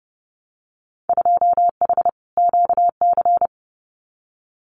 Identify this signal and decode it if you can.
Morse code, heard as a keyed tone, about 30 wpm, 700 Hz. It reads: 2H QC
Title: Winter Field Day Exchanges Encore 30wpm